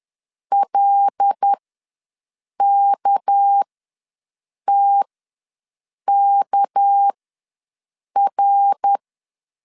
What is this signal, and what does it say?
Morse code, heard as a keyed tone, about 10 wpm, spaced as for 6 wpm, 800 Hz. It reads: LKTKR
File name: LEZIONI / 6 PKX0 / 6 PK ws 10wpm 3745